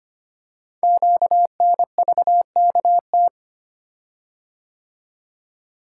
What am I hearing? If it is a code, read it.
Morse code, heard as a keyed tone, about 25 wpm, 700 Hz. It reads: QNVKT
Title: Code Groups - 5 Letter:Numbers - 25wpm